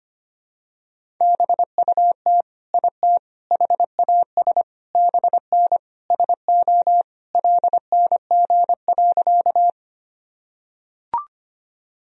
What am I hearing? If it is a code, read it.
Morse code, heard as a keyed tone, about 25 wpm, 700 Hz. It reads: BUT IT HAS BN SO LNG.
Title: Sentences from Top 100 Words and Abbreviations 25wpm